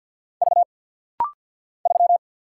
Morse code, heard as a keyed tone, about 50 wpm, 700 Hz. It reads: V 3
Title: Single Letter-Number - Rapid-Fire 50wpm